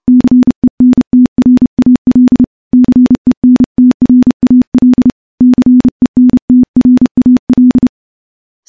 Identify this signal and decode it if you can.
Morse code, heard as a keyed tone, about 30 wpm, 250 Hz. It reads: CENTRAL CENTRAL CENTRAL
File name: morse / morse-training-30WPM